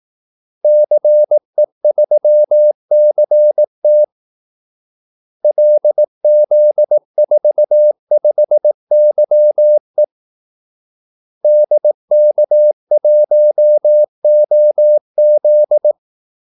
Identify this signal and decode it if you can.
Morse code, heard as a keyed tone, about 18 wpm, 600 Hz. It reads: CE3CT LZ45YE DK1OZ
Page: Kaldesignaler 18 wpm